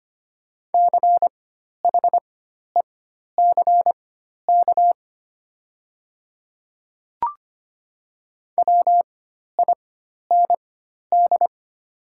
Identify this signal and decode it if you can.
Morse code, heard as a keyed tone, about 25 wpm, spaced as for 12 wpm, 700 Hz. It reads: CHECK WIND
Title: 4x Character Spacing - Top 500 Words - 25wpm